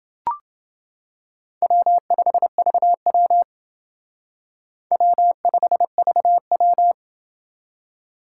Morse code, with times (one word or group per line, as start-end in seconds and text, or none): 1.6-3.4 W5VW
4.9-6.9 W5VW